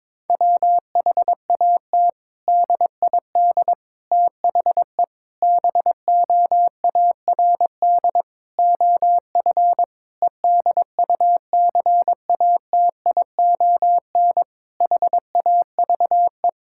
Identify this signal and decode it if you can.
Morse code, heard as a keyed tone, about 22 wpm, 700 Hz. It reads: WHAT DID THE BOARD OF EDUCATION HAVE
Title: Sentences from Top 1000 Words - Review Part 6 - 22wpm